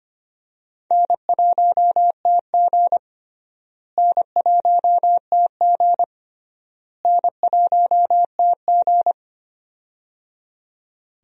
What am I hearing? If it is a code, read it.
Morse code, heard as a keyed tone, about 25 wpm, 700 Hz. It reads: N1TG N1TG N1TG
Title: USA Call Signs - 4 Characters - Repeated 3x - 25wpm